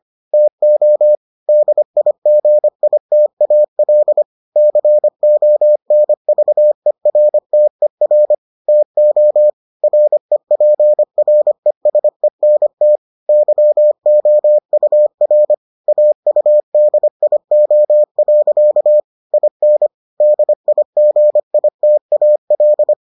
Morse code, heard as a keyed tone, about 25 wpm, 600 Hz. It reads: TO DIGITAL CONVERTER TO REPRESENT YOUR AUDIO. IN DIGITAL